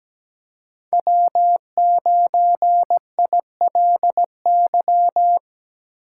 Morse code, heard as a keyed tone, about 17 wpm, 700 Hz. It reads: W9ILY
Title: K1USN SST QSO Practice - 17wpm